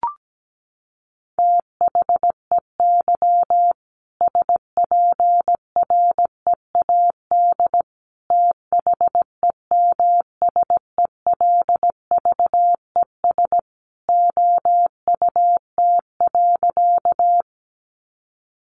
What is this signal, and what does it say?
Morse code, heard as a keyed tone, about 17 wpm, 700 Hz. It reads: THEY SPREAD THEMSELVES OUT.
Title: Sentences from Top 1200 Words 17wpm